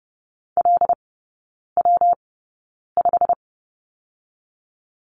Morse code, heard as a keyed tone, about 30 wpm, 700 Hz. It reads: L W 5
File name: Lesson 19e - Letter w - Three Characters 30wpm